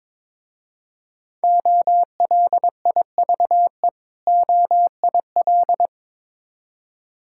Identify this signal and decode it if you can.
Morse code, heard as a keyed tone, about 22 wpm, 690 Hz. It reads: OLIVE OIL
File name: Difficult Words with Hints - 22wpm